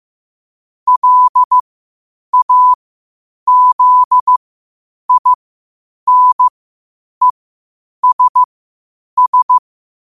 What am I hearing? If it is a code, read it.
Morse code, heard as a keyed tone, about 15 wpm, spaced as for 8 wpm, 1000 Hz. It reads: LAZINESS
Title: 8 WPM morse code quotes for Thu, 29 Feb 2024